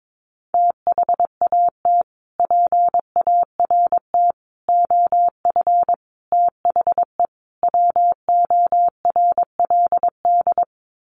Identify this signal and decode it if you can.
Morse code, heard as a keyed tone, about 22 wpm, 700 Hz. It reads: THAT PART OF THE WORLD
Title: Sets of 5 Words Encore2 22wpm